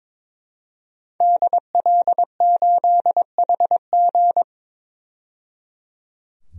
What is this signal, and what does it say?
Morse code, heard as a keyed tone, about 21 wpm, 700 Hz. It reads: DL8HG
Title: DXCC-100 Call Signs - 5 Characters - 22wpm